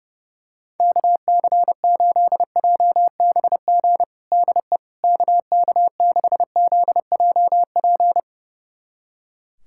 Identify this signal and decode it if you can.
Morse code, heard as a keyed tone, about 30 wpm, 700 Hz. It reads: KC8JBG DE KK6ZJP
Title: USA Callsign DE Callsign - 6 Characters - 30wpm